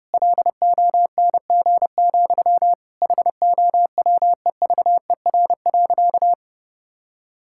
Morse code, heard as a keyed tone, about 32 wpm, 700 Hz. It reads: LONG, HOWEVER.
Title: Sentences from Top 1000 Words - Review Part 6 - 30wpm